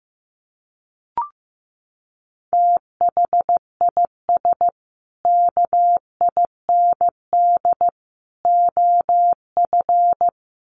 Morse code, heard as a keyed tone, about 15 wpm, 700 Hz. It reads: THIS KIND OF
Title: Sets of 3 Words 15wpm